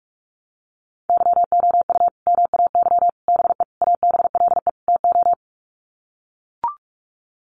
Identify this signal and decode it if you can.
Morse code, heard as a keyed tone, about 44 wpm, 700 Hz. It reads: YOU MAY BE ABLE TO
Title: Sets of 5 Words Encore 45wpm